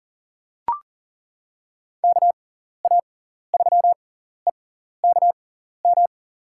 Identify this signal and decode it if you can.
Morse code, heard as a keyed tone, about 40 wpm, spaced as for 14 wpm, 700 Hz. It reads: KA3EKM
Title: USA Call Signs - 6 Characters - 6x Character Spacing - 40wpm